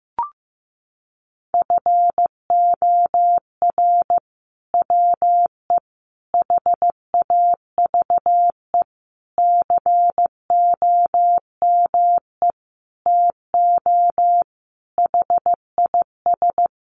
Morse code, heard as a keyed tone, about 15 wpm, 700 Hz. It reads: FOR WE HAVE COME TO HIS
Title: Sentences from Top 1000 Words 15wpm